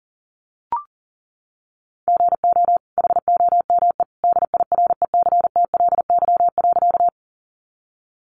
Morse code, heard as a keyed tone, about 40 wpm, 700 Hz. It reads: GO HOME DIRECTLY.
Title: Sentences from Top 800 Words 40wpm